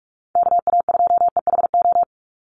Morse code, heard as a keyed tone, about 44 wpm, 700 Hz. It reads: KA2EHO
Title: USA Call Signs 45wpm